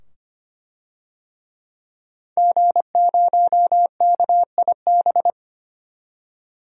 Morse code, heard as a keyed tone, about 25 wpm, 700 Hz. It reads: G0KIB
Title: DXCC-100 Call Signs - 5 Characters - 25wpm